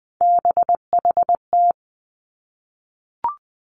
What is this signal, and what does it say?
Morse code, heard as a keyed tone, about 20 wpm, 700 Hz. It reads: BHT
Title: Three Challenging Letter-Number Combinations 20wpm